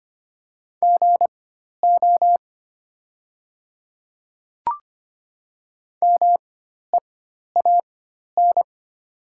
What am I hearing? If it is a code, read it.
Morse code, heard as a keyed tone, about 25 wpm, spaced as for 12 wpm, 700 Hz. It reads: GO MEAN
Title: 4x Character Spacing - Top 300 Words - 25wpm